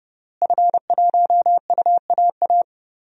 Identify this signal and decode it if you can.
Morse code, heard as a keyed tone, about 30 wpm, 700 Hz. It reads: F1UAA
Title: DXCC-100 Call Signs - 5 Characters - Repeated 3x - 30wpm